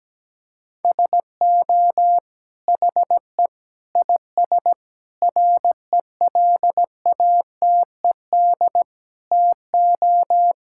Morse code, heard as a keyed tone, about 17 wpm, 700 Hz. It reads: SO HE IS RELATED TO
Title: Sentences from Top 1000 Words - Review Part 5 - 17wpm